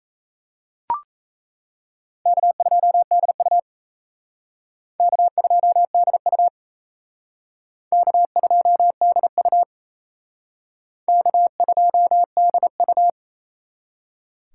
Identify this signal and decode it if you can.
Morse code, heard as a keyed tone, about 35 wpm, 700 Hz. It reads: K2DU K2DU K2DU K2DU
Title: Speed Racing - USA 1x2 Call Signs 28wpm